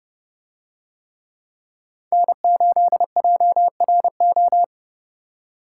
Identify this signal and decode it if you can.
Morse code, heard as a keyed tone, about 29 wpm, 700 Hz. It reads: N8JRO